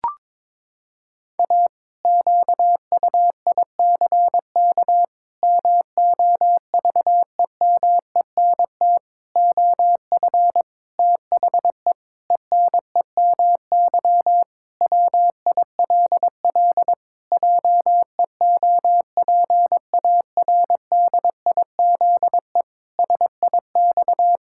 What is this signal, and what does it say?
Morse code, heard as a keyed tone, about 22 wpm, 700 Hz. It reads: A QUICK MOVEMENT OF THE ENEMY WILL JEOPARDIZE SIX
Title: Pangram Challenge v2 - 22wpm